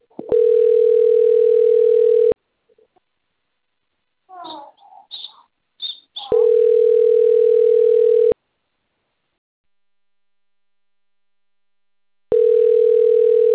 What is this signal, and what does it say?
Ringback tone (North America), 440 + 480 Hz.
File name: rg-1600-unknown- - - .57249